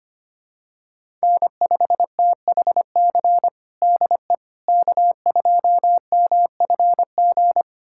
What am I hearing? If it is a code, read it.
Morse code, heard as a keyed tone, about 25 wpm, 700 Hz. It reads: N5THC DE K2MFG
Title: USA Callsign DE Callsign - 5 Characters - 25wpm